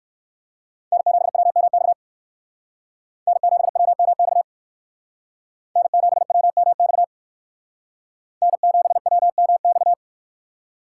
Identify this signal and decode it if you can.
Morse code, heard as a keyed tone, about 55 wpm, 700 Hz. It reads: N7WMX N7WMX N7WMX N7WMX
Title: Speed Racing - USA 1x3 Call Signs 45wpm